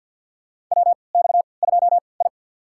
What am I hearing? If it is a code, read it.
Morse code, heard as a keyed tone, about 53 wpm, spaced as for 29 wpm, 700 Hz. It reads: WX2I
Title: USA Call Signs - 3x Character Spacing - 50wpm